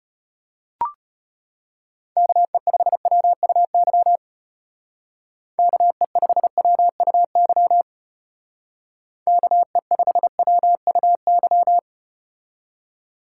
Morse code, heard as a keyed tone, about 33 wpm, 700 Hz. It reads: KE5WUY KE5WUY KE5WUY